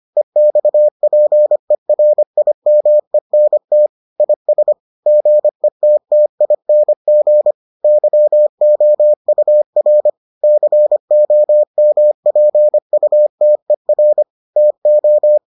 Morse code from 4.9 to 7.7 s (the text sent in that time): GETTING